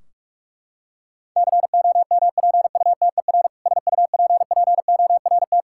Morse code, heard as a keyed tone, about 46 wpm, 700 Hz. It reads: COMPUTER SUPPORT
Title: Practice QSO - Exchange 3 - More Info - 45wpm